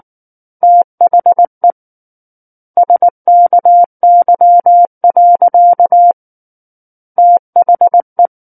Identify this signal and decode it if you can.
Morse code, heard as a keyed tone, about 19 wpm, 700 Hz. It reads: THE SKY. THE
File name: Book One / 10-Chapter 10 - 19 WPM